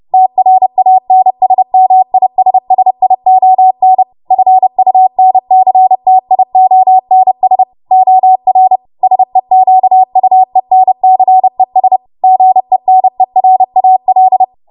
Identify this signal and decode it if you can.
Morse code, heard as a keyed tone, about 30 wpm, 750 Hz. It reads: TRANSMISSION FUNCTIONS OR SEQUENCES GENERAL
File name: Morse / 30WPM